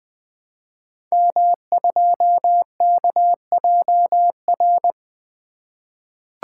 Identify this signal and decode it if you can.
Morse code, heard as a keyed tone, about 20 wpm, 700 Hz. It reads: M2KJR
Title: Code Groups - 5 Letter:Numbers - 20wpm